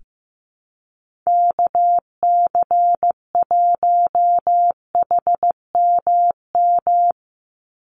Morse code, heard as a keyed tone, about 15 wpm, 700 Hz. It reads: KC1HMM
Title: USA Call Signs 15wpm